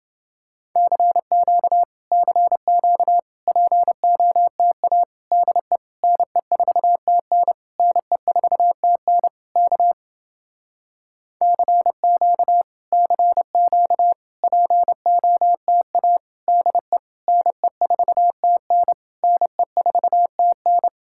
Morse code, heard as a keyed tone, about 30 wpm, 700 Hz. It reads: CQ CQ POTA DE NE4TN NE4TN K CQ CQ POTA DE NE4TN NE4TN